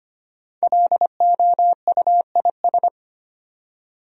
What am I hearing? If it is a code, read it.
Morse code, heard as a keyed tone, about 25 wpm, 700 Hz. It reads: LOUIS
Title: USA Common Names - 25wpm